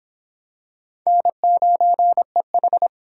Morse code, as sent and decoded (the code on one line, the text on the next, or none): -. ----. ....
N9EH